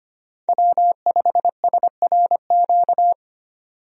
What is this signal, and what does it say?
Morse code, heard as a keyed tone, about 25 wpm, 700 Hz. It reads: W5SRQ
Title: USA Call Signs - 5 Characters - 25wpm